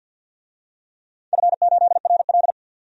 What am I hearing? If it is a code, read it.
Morse code, heard as a keyed tone, about 48 wpm, 700 Hz. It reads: F8RL
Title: DXCC-100 Call Signs - 4 Characters - 50wpm